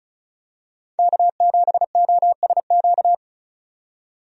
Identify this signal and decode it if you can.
Morse code, heard as a keyed tone, about 34 wpm, 700 Hz. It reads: K7OSQ
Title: USA Call Signs - 5 Characters - 35wpm